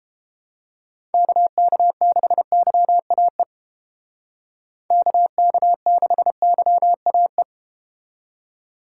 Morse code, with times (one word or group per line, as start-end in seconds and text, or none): 1.1-3.4 KK6YAE
4.9-7.4 KK6YAE